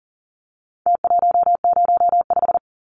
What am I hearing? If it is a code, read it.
Morse code, heard as a keyed tone, about 40 wpm, 700 Hz. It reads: T105
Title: Numbers - Contest Style - 4-Digit 40wpm